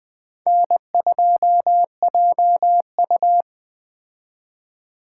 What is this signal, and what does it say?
Morse code, heard as a keyed tone, about 20 wpm, 700 Hz. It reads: N2JU
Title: Speed Racing - USA 1x2 Call Signs 17wpm